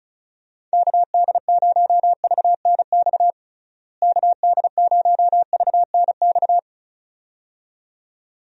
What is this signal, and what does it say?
Morse code, heard as a keyed tone, about 36 wpm, 700 Hz. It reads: KD0VNX KD0VNX
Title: USA Call Signs - 6 Characters - Repeated 2x - 35wpm